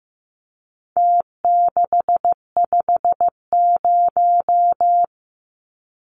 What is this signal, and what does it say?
Morse code, heard as a keyed tone, about 15 wpm, 700 Hz. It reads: T650
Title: Numbers - Contest Style - 4-Digit 15wpm